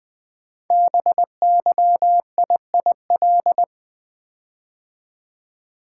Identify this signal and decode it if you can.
Morse code, heard as a keyed tone, about 20 wpm, 700 Hz. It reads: BYIIL
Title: Code Groups - 5 Letters - 20wpm